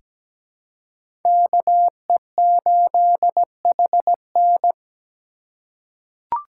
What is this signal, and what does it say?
Morse code, heard as a keyed tone, about 17 wpm, 700 Hz. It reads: KE8HN
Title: USA Call Signs - 4x Character Spacing - 17wpm